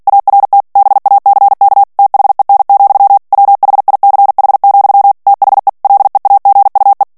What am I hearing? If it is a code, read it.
Morse code, heard as a keyed tone, about 48 wpm, 800 Hz. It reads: ART BACK THEN, WHICH, THE LEAGUE
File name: ANGLAIS texte clair 3 vitesse 45 mots minute